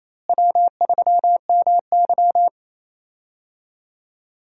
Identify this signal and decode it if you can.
Morse code, heard as a keyed tone, about 28 wpm, 700 Hz. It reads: W3MY
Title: USA Call Signs - 4 Characters - Repeated 3x - 28wpm